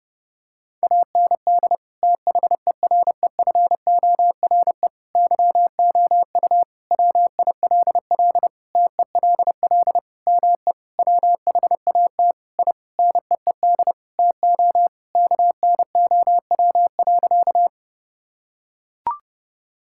Morse code, heard as a keyed tone, about 32 wpm, 700 Hz. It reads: AND THEREFORE YOU WILL TELL ME WHAT I NEED TO KNOW.
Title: Sentences from Top 1000 Words - Review Part 5 - 30wpm